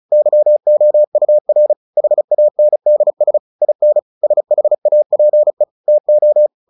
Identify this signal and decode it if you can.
Morse code, heard as a keyed tone, about 34 wpm, 600 Hz. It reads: YOUR HANDS IN SHAPE TO
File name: the old man and the sea 2 35wpm